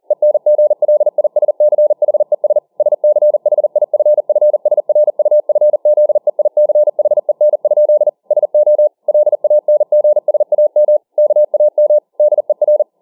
Morse code, heard as a keyed tone, about 40 wpm, 600 Hz. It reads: ENGLISCHES SCHIFFSRUFZEICHEN? SO LANGSAM KAM DER